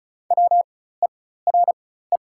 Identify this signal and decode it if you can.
Morse code, heard as a keyed tone, about 36 wpm, spaced as for 16 wpm, 700 Hz. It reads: WERE
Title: 4x Character Spacing - Top 100 Words - 35wpm